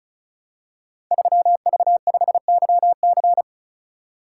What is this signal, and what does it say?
Morse code, heard as a keyed tone, about 36 wpm, 700 Hz. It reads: 3V5YC
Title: Code Groups - 5 Letter:Numbers - 35wpm